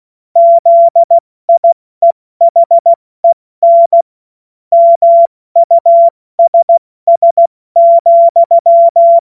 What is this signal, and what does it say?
Morse code, heard as a keyed tone, about 16 wpm, 680 Hz. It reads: ZIEHEN MUSS,
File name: Morsekurs / Witze / Witz 07. Dezember 2021, 70 F80